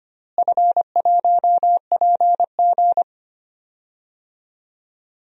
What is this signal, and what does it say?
Morse code, heard as a keyed tone, about 25 wpm, 700 Hz. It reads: F1PG
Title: DXCC-100 Call Signs - 4 Characters - Repeated 2x - 25wpm